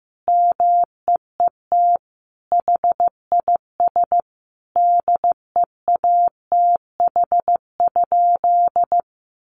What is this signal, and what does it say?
Morse code, heard as a keyed tone, about 15 wpm, 700 Hz. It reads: MEET HIS DEATH?
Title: Sentences from Top 1100 Words 15wpm